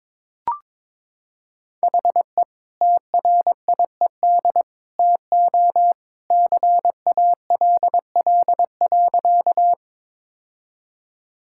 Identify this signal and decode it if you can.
Morse code, heard as a keyed tone, about 22 wpm, 700 Hz. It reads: HE TRIED TO CALL.
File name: practice / 22wpm / Sentences with 2-Word Phrases - 22wpm